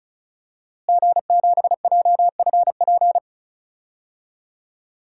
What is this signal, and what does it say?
Morse code, heard as a keyed tone, about 35 wpm, 700 Hz. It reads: G7JFP
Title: DXCC-100 Call Signs - 5 Characters - Repeated 2x - 35wpm